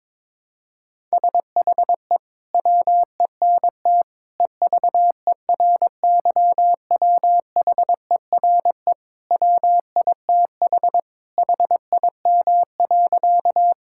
Morse code, heard as a keyed tone, about 21 wpm, 700 Hz. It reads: SHE WENT EVERYWHERE WITH HIM.
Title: Sentences from Top 1600 Words 22wpm